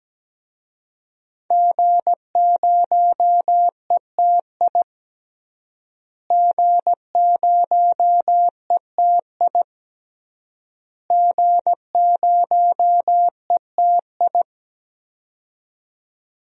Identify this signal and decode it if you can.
Morse code, heard as a keyed tone, about 17 wpm, 700 Hz. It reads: G0ETI G0ETI G0ETI
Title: DXCC-100 Call Signs - 5 Characters - Repeated 3x - 17wpm